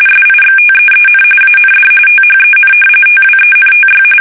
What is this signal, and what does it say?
RTTY, about 45.45 baud. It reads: SIXTEEN AND THEN TWENTY
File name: fsk850 / battleshipnj